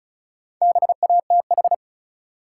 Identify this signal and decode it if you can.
Morse code, heard as a keyed tone, about 36 wpm, 700 Hz. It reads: BATH
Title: Sentences from Top 1600 Words - New Word Drill - 35wpm